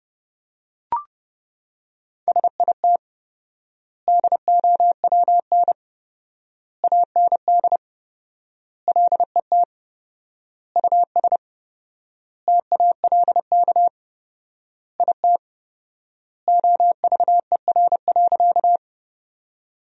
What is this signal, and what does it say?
Morse code, heard as a keyed tone, about 30 wpm, 700 Hz. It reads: SIT DOWN AND LET US TALK IT OVER.
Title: 4x Word-Spacing - Sentences from Top 500 Words - 30wpm